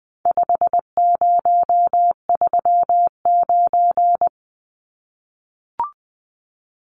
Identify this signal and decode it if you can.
Morse code, heard as a keyed tone, about 20 wpm, 700 Hz. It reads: 5039
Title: Numbers - 4-Digit 20wpm